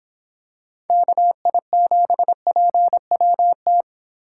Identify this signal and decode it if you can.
Morse code, heard as a keyed tone, about 26 wpm, 700 Hz. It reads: KI7PWT